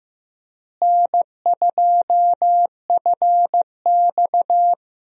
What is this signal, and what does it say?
Morse code, heard as a keyed tone, about 15 wpm, 700 Hz. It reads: N2FX
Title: USA Call Signs - 4 Characters - Repeated 2x - 15wpm